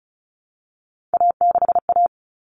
Morse code, heard as a keyed tone, about 34 wpm, 700 Hz. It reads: A6A
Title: Common Cut Numbers - 3-Digit 35wpm